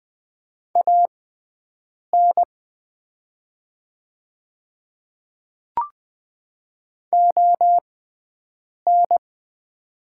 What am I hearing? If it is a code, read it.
Morse code, heard as a keyed tone, about 20 wpm, spaced as for 7 wpm, 690 Hz. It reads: AN ON
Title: 6x Character Spacing - Top 200 Words - 20wpm